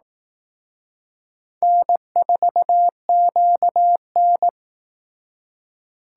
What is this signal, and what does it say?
Morse code, heard as a keyed tone, about 18 wpm, 700 Hz. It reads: N4QN
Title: Speed Racing - USA 1x2 Call Signs 15wpm